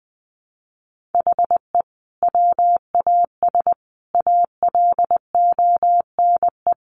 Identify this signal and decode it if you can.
Morse code, heard as a keyed tone, about 20 wpm, 700 Hz. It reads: HE WAS ALONE
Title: Sentences from Top 1000 Words 20wpm